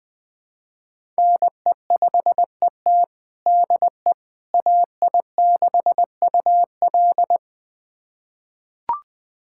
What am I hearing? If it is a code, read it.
Morse code, heard as a keyed tone, about 20 wpm, 700 Hz. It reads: NE5ET DE AI6UL
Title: USA Callsign DE Callsign - 5 Characters - 20wpm